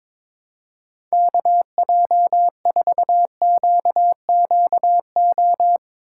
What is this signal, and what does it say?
Morse code, heard as a keyed tone, about 22 wpm, 700 Hz. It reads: KJ4QQO
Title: USA Call Signs - 6 Characters - 2x Character Spacing - 22wpm